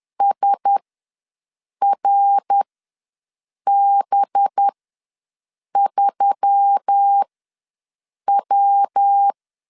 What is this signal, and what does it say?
Morse code, heard as a keyed tone, about 10 wpm, spaced as for 6 wpm, 800 Hz. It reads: SRB3W
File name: LEZIONI / 5 LBVC / 5 LB ws 10wpm 3616